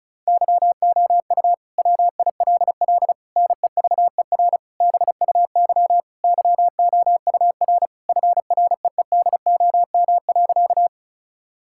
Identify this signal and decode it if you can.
Morse code, heard as a keyed tone, about 36 wpm, 700 Hz. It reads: YOU WILL NEVER BUY YOUR FREEDOM.
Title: Sentences from Top 1000 Words - Review Part 5 - 35wpm